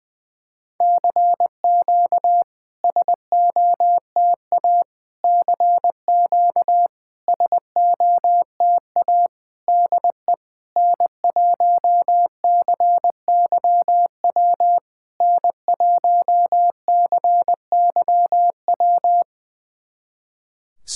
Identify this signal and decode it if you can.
Morse code, heard as a keyed tone, about 20 wpm, 700 Hz. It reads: CQ SOTA CQ SOTA DE N1CYW N1CYW
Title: SOTA QSO Practice 20wpm